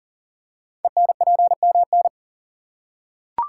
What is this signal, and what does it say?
Morse code, heard as a keyed tone, about 40 wpm, 700 Hz. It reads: ENPMN
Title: Code Groups - 5 Letter:Numbers - 40wpm